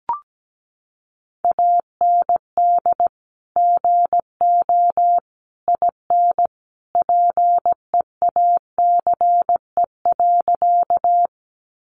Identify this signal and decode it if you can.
Morse code, heard as a keyed tone, about 17 wpm, 700 Hz. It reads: AND GO IN PEACE.
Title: Sentences from Top 1500 Words 17wpm